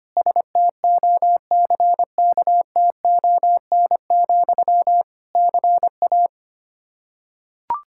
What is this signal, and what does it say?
Morse code, heard as a keyed tone, about 25 wpm, 700 Hz. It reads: STOCKTON, CA
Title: Practice QSO - Exchange 1 - Beginning - 25wpm